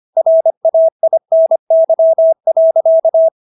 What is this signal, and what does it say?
Morse code, heard as a keyed tone, about 25 wpm, 650 Hz. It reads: RAINY.